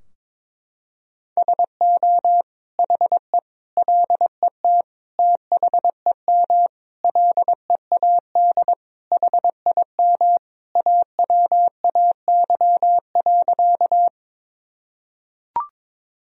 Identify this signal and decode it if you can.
Morse code, heard as a keyed tone, about 21 wpm, 700 Hz. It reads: SO HE LET THEM LEAD HIM AWAY.
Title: Sentences from Top 1000 Words - Review Part 2 - 22wpm